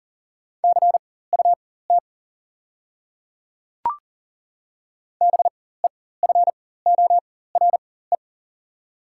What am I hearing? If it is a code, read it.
Morse code, heard as a keyed tone, about 40 wpm, spaced as for 19 wpm, 690 Hz. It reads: CUT BEFORE
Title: 4x Character Spacing - Top 500 Words - 40wpm